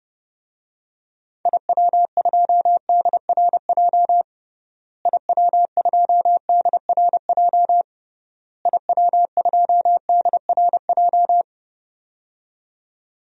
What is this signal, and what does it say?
Morse code, heard as a keyed tone, about 32 wpm, 700 Hz. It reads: IW2DRJ IW2DRJ IW2DRJ
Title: DXCC-100 Call Signs - 6 Characters - Repeated 3x - 30wpm